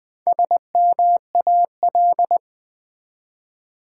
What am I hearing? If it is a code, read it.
Morse code, heard as a keyed tone, about 20 wpm, 700 Hz. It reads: SMAL
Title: Build A Word - 1 Letter At A Time - 3 to 5-Letter Words - 20wpm